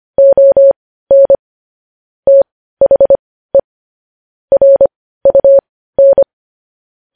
Morse code, heard as a keyed tone, about 25 wpm, spaced as for 15 wpm, 550 Hz. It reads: ON THE RUN
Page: news headlines in morse code at 15 wpm